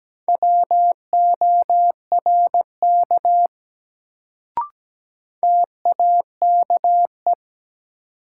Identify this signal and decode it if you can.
Morse code, heard as a keyed tone, about 17 wpm, 700 Hz. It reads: WORK TAKE
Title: Sentences from Top 200 Words - New Word Drill - 17wpm